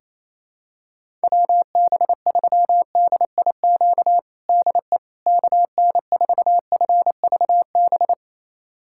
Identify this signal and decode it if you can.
Morse code, heard as a keyed tone, about 28 wpm, 700 Hz. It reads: WB3DIQ DE KN4FVB
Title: USA Callsign DE Callsign - 6 Characters - 28wpm